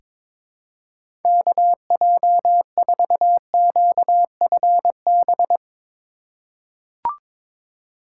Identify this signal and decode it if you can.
Morse code, heard as a keyed tone, about 22 wpm, 700 Hz. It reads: KJ4QFB
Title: USA Call Signs - 2x Character Spacing - 22wpm